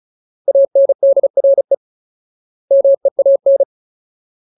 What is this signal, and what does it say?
Morse code, heard as a keyed tone, about 34 wpm, 550 Hz. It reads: ANDRE MEAN
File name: Ultimate Word List / 35WPM Ultimate